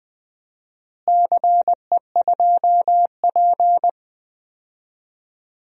Morse code, heard as a keyed tone, about 20 wpm, 700 Hz. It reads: CE2P
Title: DXCC-100 Call Signs - 4 Characters - Repeated 3x - 20wpm